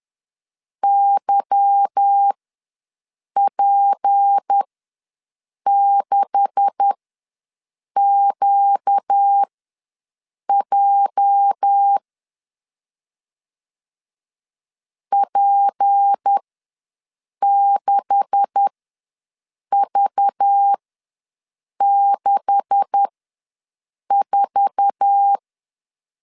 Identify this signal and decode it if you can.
Morse code, heard as a keyed tone, about 10 wpm, spaced as for 6 wpm, 800 Hz. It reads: YP6QJ P6V64
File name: LEZIONI / 9 286 ws 10wpm 4032